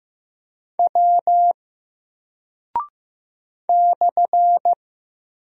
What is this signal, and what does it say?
Morse code, heard as a keyed tone, about 15 wpm, 700 Hz. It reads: W /
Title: Single Character (Letter:Number:Punctuation:Prosign) 15wpm